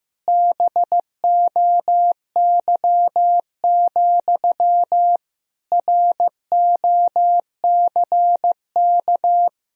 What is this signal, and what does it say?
Morse code, heard as a keyed tone, about 15 wpm, 700 Hz. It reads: BOY, ROCK
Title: Sentences from Top 1000 Words - Review Part 2 - 15wpm